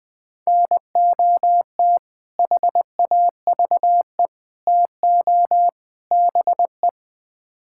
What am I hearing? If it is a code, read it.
Morse code, heard as a keyed tone, about 20 wpm, 700 Hz. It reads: NOT HAVE TO BE
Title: Build A Phrase - 4-Word Phrases - 20wpm